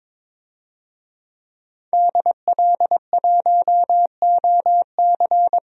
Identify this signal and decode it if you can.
Morse code, heard as a keyed tone, about 22 wpm, 700 Hz. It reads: DL1OC
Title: DXCC-100 Call Signs - 5 Characters - 22wpm